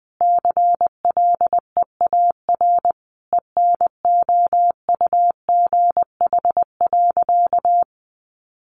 Morse code, heard as a keyed tone, about 20 wpm, 700 Hz. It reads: CLEAR ENOUGH.
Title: Sentences from Top 1500 Words 20wpm